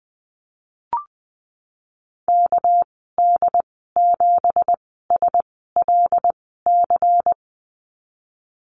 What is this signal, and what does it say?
Morse code, heard as a keyed tone, about 20 wpm, spaced as for 15 wpm, 700 Hz. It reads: KD7SLC
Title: 231 Lesson 37g - Letter x - Call Signs 20wpm